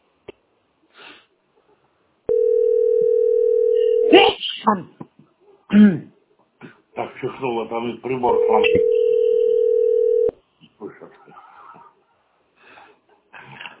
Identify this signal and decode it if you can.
Ringback tone (North America), 440 + 480 Hz, with speech.
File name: 625ed125-ef38-47d7-b74d